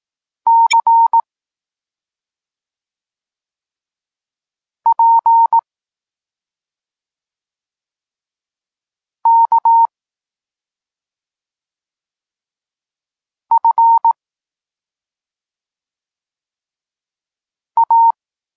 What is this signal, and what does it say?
Morse code, heard as a keyed tone, about 18 wpm, 920 Hz. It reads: C P K F A